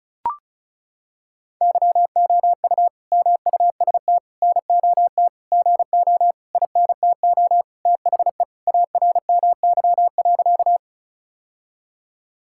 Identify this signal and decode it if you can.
Morse code, heard as a keyed tone, about 36 wpm, 700 Hz. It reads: YOU MUST NOT GO INTO THE ARMY.
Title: Sentences from Top 1000 Words - Review Part 5 - 35wpm